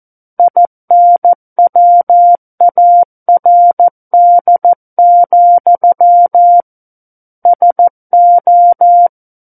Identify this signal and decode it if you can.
Morse code, heard as a keyed tone, about 14 wpm, 700 Hz. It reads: INWARD, SO
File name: Book One / 05-Chapter 5 - 14 WPM